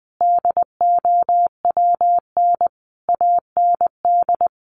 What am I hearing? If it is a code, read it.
Morse code, heard as a keyed tone, about 20 wpm, 700 Hz. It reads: DOWN AND
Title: A Sound Of Thunder 20wpm